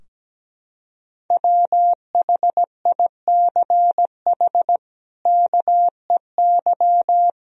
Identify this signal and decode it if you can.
Morse code, heard as a keyed tone, about 17 wpm, 700 Hz. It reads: WHICH KEY
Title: Sentences from Top 1000 Words - Review Part 4 - 17wpm